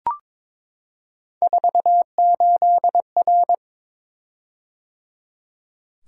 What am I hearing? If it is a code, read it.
Morse code, heard as a keyed tone, about 22 wpm, 700 Hz. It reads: 48R
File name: practice / 22wpm / Three Letter-Number Combinations - 22wpm